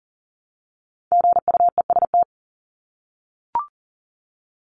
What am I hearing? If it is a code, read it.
Morse code, heard as a keyed tone, about 40 wpm, 700 Hz. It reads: GUEST
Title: Top 1501-1750 Words 40wpm